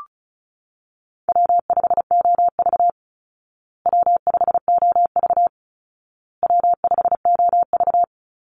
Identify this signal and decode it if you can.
Morse code, heard as a keyed tone, about 34 wpm, 700 Hz. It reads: W5OV W5OV W5OV
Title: Easy Call Signs Repeated 3x - 35wpm